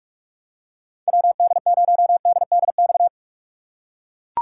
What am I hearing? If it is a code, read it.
Morse code, heard as a keyed tone, about 43 wpm, 700 Hz. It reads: WD0DDX
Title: USA Call Signs - 3x Character Spacing - 45wpm